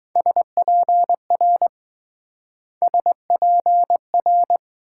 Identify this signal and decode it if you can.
Morse code, heard as a keyed tone, about 22 wpm, 700 Hz. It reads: SPR SPR